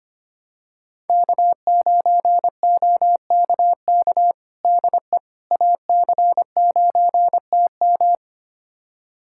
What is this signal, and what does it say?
Morse code, heard as a keyed tone, about 25 wpm, 700 Hz. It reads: K9OKK DE AC9TM
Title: USA Callsign DE Callsign - 5 Characters - 25wpm